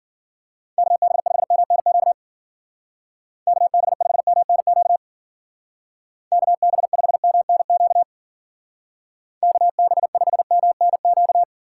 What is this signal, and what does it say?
Morse code, heard as a keyed tone, about 51 wpm, 700 Hz. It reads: KB5MNQ KB5MNQ KB5MNQ KB5MNQ